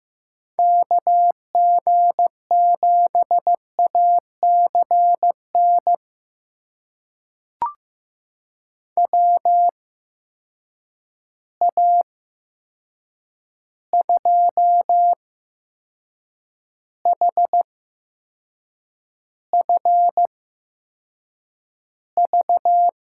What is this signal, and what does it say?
Morse code, heard as a keyed tone, about 15 wpm, spaced as for 6 wpm, 700 Hz. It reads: KG7ACN WA2HFV
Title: USA Call Signs - 6 Characters - 8x Character Spacing - 15wpm